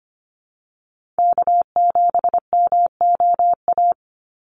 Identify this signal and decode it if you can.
Morse code, heard as a keyed tone, about 25 wpm, 700 Hz. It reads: K7MOA
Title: Call Signs - 25wpm